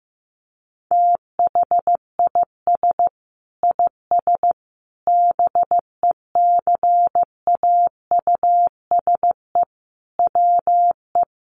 Morse code, heard as a keyed tone, about 15 wpm, 700 Hz. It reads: THIS IS BECAUSE WE
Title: Sentences from Top 1200 Words 15wpm